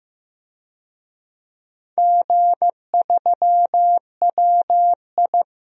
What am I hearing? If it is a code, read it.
Morse code, heard as a keyed tone, about 15 wpm, 700 Hz. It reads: G3WI
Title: DXCC-100 Call Signs - 4 Characters - Repeated 2x - 15wpm